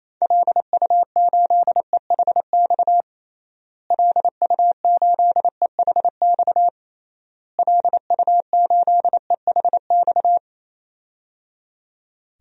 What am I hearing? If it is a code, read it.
Morse code, heard as a keyed tone, about 28 wpm, 700 Hz. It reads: LU8EHX LU8EHX LU8EHX
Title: DXCC-100 Call Signs - 6 Characters - Repeated 3x - 28wpm